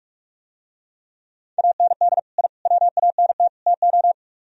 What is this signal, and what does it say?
Morse code, heard as a keyed tone, about 46 wpm, 700 Hz. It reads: AND I WANT TO